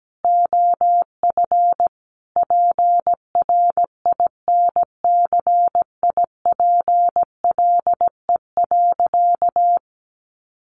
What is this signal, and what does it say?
Morse code, heard as a keyed tone, about 17 wpm, 700 Hz. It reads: OF PRINCIPLE.
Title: Sentences from Top 1400 Words 17wpm